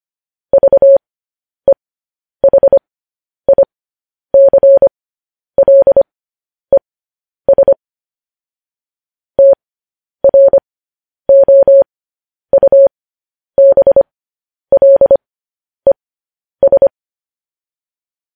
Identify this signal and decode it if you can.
Morse code, heard as a keyed tone, about 25 wpm, spaced as for 10 wpm, 550 Hz. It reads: VEHICLES TROUBLES
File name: podcast / News headlines-1x-25x10